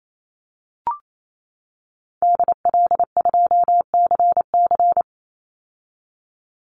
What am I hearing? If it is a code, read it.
Morse code, heard as a keyed tone, about 28 wpm, 700 Hz. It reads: DL2CC
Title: Call Signs - 28wpm